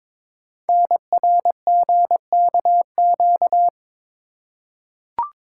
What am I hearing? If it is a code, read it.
Morse code, heard as a keyed tone, about 22 wpm, 700 Hz. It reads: NRGKQ
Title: Code Groups - 5 Letters - 22wpm